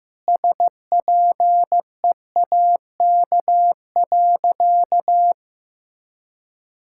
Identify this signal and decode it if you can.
Morse code, heard as a keyed tone, about 15 wpm, 700 Hz. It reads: SPEAK.
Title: Sentences from Top 1600 Words 15wpm